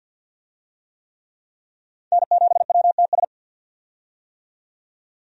Morse code, heard as a keyed tone, about 46 wpm, 700 Hz. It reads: N7WTS
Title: USA Call Signs - 5 Characters - 50wpm